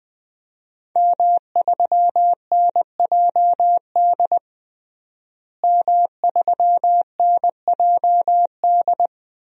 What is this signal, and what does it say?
Morse code, heard as a keyed tone, about 21 wpm, 700 Hz. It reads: M3NJD M3NJD
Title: DXCC-100 Call Signs - 5 Characters - Repeated 2x - 20wpm